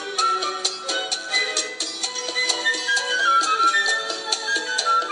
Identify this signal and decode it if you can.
Emergency Alert System (SAME) tones, heard as harsh buzzing bursts. It is heard from 1.2 to 4.8 s.